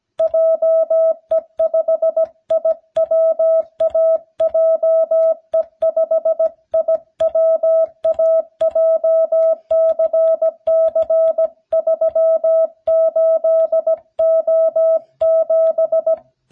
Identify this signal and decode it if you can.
Morse code, heard as a keyed tone, about 17 wpm, 650 Hz. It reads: JE5IWA JE5IWA JCC38O7